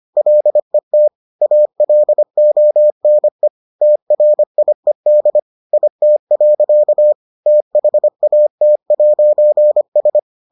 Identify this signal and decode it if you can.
Morse code, heard as a keyed tone, about 25 wpm, 600 Hz. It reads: LET ALONE TRIED IT. THAT'S